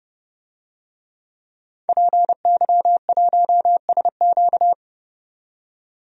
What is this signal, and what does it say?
Morse code, heard as a keyed tone, about 30 wpm, 700 Hz. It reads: PY1SQ
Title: DXCC-100 Call Signs - 5 Characters - 30wpm